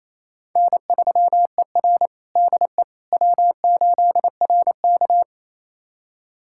Morse code, heard as a keyed tone, about 28 wpm, 700 Hz. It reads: N3ER DE W8RK
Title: USA Callsign DE Callsign - 4 Characters - 28wpm